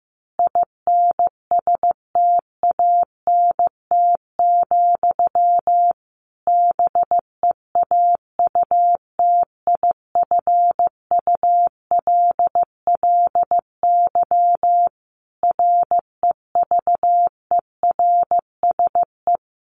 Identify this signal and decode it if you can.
Morse code, heard as a keyed tone, about 15 wpm, 700 Hz. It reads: INSTANT, BEAUTIFULLY REVERSE
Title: A Sound Of Thunder 15wpm